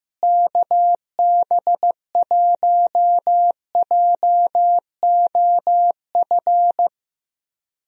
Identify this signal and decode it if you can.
Morse code, heard as a keyed tone, about 15 wpm, 700 Hz. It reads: KB1JOF
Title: USA Call Signs - 6 Characters - Repeated 2x - 15wpm